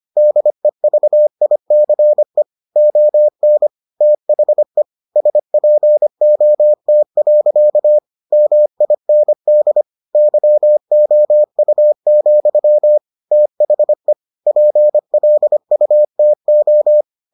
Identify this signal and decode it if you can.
Morse code, heard as a keyed tone, about 25 wpm, 600 Hz. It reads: DEVICE ON THE SPOT. MIND YOU, THE PLUTO